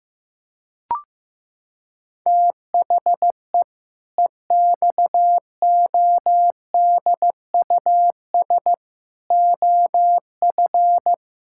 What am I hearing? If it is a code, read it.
Morse code, heard as a keyed tone, about 15 wpm, 700 Hz. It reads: THE EXODUS OF
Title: Pangram Challenge v2 - 15wpm